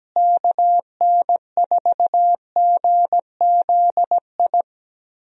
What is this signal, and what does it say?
Morse code, heard as a keyed tone, about 17 wpm, 700 Hz. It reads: KN4GZI
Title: USA Call Signs - 6 Characters - Repeated 3x - 17wpm